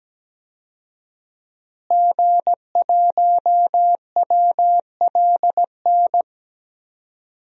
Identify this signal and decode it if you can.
Morse code, heard as a keyed tone, about 17 wpm, 700 Hz. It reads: G1WLN
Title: DXCC-100 Call Signs - 5 Characters - Repeated 3x - 17wpm